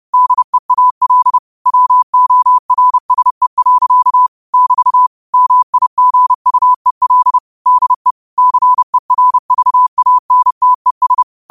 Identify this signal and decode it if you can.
Morse code, heard as a keyed tone, about 30 wpm, 1000 Hz. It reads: DEAL WORSE. <BT> MIGUEL DE CERVANTES
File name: qotdfiles / 30wpm